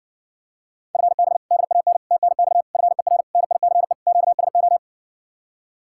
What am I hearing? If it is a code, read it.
Morse code, heard as a keyed tone, about 57 wpm, 700 Hz. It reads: FB DAN TNX FER NICE QSO